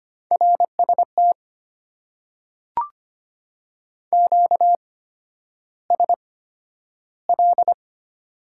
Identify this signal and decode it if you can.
Morse code, heard as a keyed tone, about 25 wpm, spaced as for 11 wpm, 690 Hz. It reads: RST QSL
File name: Top 90 QSO Elements - 8x Character Spacing - 25wpm